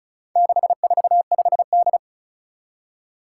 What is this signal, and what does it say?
Morse code, heard as a keyed tone, about 36 wpm, 700 Hz. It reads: 645D
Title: Four Challenging Letter-Number Combinations 35wpm